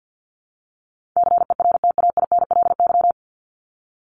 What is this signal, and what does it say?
Morse code, heard as a keyed tone, about 50 wpm, 700 Hz. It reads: CERTAINLY
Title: Top 751-1000 Words 50wpm